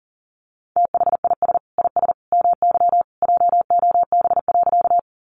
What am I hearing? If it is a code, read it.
Morse code, heard as a keyed tone, about 37 wpm, 700 Hz. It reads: THIS IS MY JOB.
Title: Sentences from Top 600 Words 40wpm